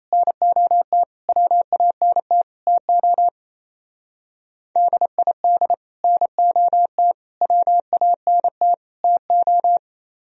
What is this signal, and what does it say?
Morse code, heard as a keyed tone, about 33 wpm, 700 Hz. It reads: NOT WANT TO DID NOT WANT TO
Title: Speed-Racing - Sets of 4 Words 28wpm